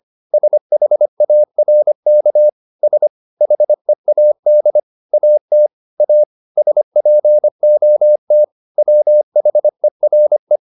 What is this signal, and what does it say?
Morse code, heard as a keyed tone, about 25 wpm, 600 Hz. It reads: SHARK S HEAD AT A SPOT WHERE